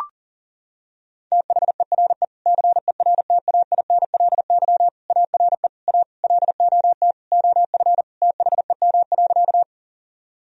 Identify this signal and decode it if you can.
Morse code, heard as a keyed tone, about 40 wpm, 700 Hz. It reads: THERE CERTAINLY ARE A LOT OF THEM.
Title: Sentences from Top 1000 Words - Review Part 3 - 40wpm